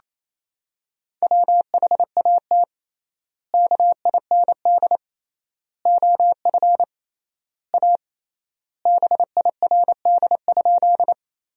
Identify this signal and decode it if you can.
Morse code, heard as a keyed tone, about 28 wpm, 700 Hz. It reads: WHAT KIND OF A BIRD?
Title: 3x Word-Spacing - Sentences from Top 400 Words - 28wpm